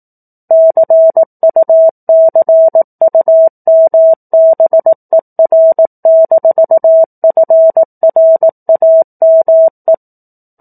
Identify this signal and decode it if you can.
Morse code, heard as a keyed tone, about 18 wpm, 650 Hz. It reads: CUCUMBER-FRAME
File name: 04-Chapter-04 18 WPM